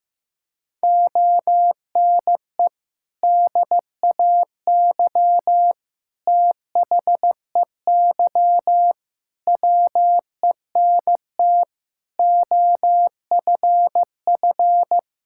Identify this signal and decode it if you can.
Morse code, heard as a keyed tone, about 15 wpm, 700 Hz. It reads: ONE DAY THEY WENT OFF
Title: Sentences from Top 1600 Words 15wpm